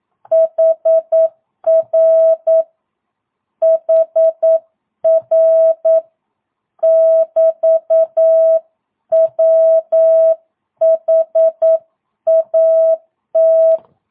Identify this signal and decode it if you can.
Morse code, heard as a keyed tone, about 9 wpm, 650 Hz. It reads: HR HR <BT> WHAT
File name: jh8cbh / 2022cw